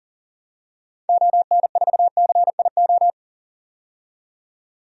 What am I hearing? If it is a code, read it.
Morse code, heard as a keyed tone, about 40 wpm, 700 Hz. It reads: ON4CIO
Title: DXCC-100 Call Signs - 6 Characters - 40wpm